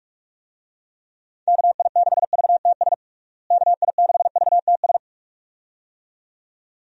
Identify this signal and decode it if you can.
Morse code, heard as a keyed tone, about 41 wpm, 700 Hz. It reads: KI6VTS KI6VTS
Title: USA Call Signs - 6 Characters - Repeated 2x - 45wpm